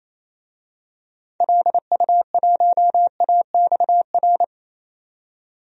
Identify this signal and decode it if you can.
Morse code, heard as a keyed tone, about 28 wpm, 700 Hz. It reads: LU1AXR
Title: DXCC-100 Call Signs - 6 Characters - 28wpm